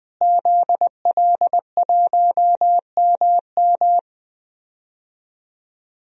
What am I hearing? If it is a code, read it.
Morse code, heard as a keyed tone, about 20 wpm, 700 Hz. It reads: ZL1MM
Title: DXCC-100 Call Signs - 5 Characters - Repeated 3x - 20wpm